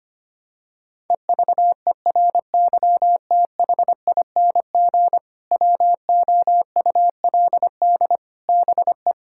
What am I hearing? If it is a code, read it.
Morse code, heard as a keyed tone, about 25 wpm, 700 Hz. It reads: EVERYTHING WOULD BE
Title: Sentences from Top 1000 Words - Review Part 5 - 25wpm